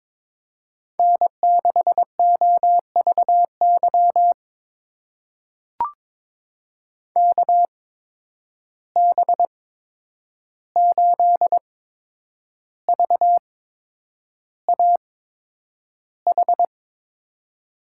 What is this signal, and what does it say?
Morse code, heard as a keyed tone, about 22 wpm, spaced as for 9 wpm, 700 Hz. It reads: N6OVY KB8VAH
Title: USA Call Signs - 8x Character Spacing - 22wpm